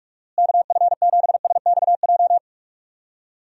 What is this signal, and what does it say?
Morse code, heard as a keyed tone, about 44 wpm, 700 Hz. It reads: KF7SXJ
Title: USA Call Signs - 6 Characters - Repeated 2x - 45wpm